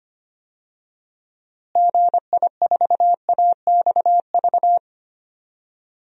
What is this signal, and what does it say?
Morse code, heard as a keyed tone, about 25 wpm, 700 Hz. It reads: GI4AXV
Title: DXCC-100 Call Signs - 6 Characters - Repeated 3x - 25wpm